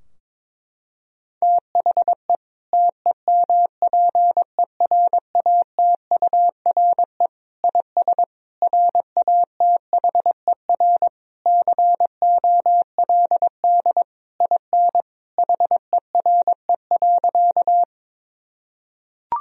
Morse code, heard as a keyed tone, about 21 wpm, 700 Hz. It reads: THE TEMPERATURE IS RATHER COLD IN HERE.
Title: Sentences from Top 1000 Words - Review Part 4 - 22wpm